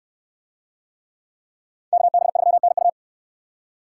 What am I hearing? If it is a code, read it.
Morse code, heard as a keyed tone, about 62 wpm, 700 Hz. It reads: KB3NU